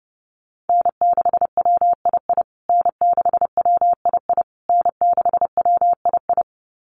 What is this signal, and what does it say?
Morse code, heard as a keyed tone, about 30 wpm, 700 Hz. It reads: N6WII N6WII N6WII
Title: USA Call Signs Repeated 3x 30wpm